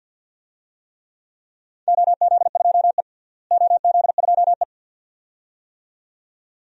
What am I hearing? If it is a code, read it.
Morse code, heard as a keyed tone, about 52 wpm, 700 Hz. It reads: OZ2E OZ2E
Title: DXCC-100 Call Signs - 4 Characters - Repeated 2x - 50wpm